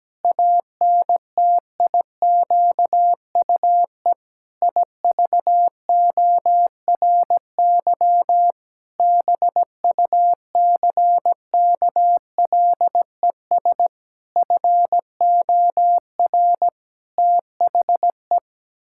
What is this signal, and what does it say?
Morse code, heard as a keyed tone, about 17 wpm, 700 Hz. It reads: ANTIQUE IVORY BUCKLES FOR THE
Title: Pangram Challenge v2 - 17wpm